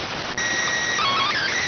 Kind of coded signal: SSTV (slow-scan television)

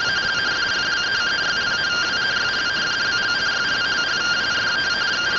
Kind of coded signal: two-tone teleprinter signal

